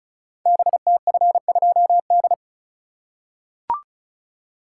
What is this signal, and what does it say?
Morse code, heard as a keyed tone, about 36 wpm, 690 Hz. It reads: BTF2D
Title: Code Groups - 5 Letter:Numbers - 35wpm